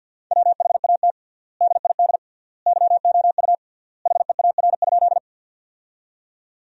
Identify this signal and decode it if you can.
Morse code, heard as a keyed tone, about 50 wpm, 700 Hz. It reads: WHAT DID YOU HEAR?
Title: 3x Word-Spacing - Sentences from Top 300 Words - 50wpm